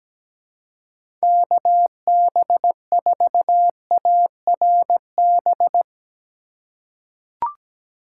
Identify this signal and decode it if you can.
Morse code, heard as a keyed tone, about 17 wpm, 700 Hz. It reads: KB4ARB
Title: USA Call Signs - 6 Characters - 3x Character Spacing - 17wpm